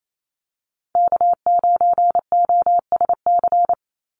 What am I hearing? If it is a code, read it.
Morse code, heard as a keyed tone, about 28 wpm, 700 Hz. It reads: K9OSC
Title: NA SOTA Chaser Callsigns 28wpm